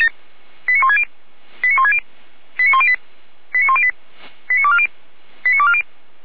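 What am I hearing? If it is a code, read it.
Five-tone selective call (ZVEI), heard as quick runs of five tones, tones of about 80 ms. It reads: 89160 89160 89108 89108 89240 89240
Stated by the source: ZVEI five-tone selective call received with an SDR.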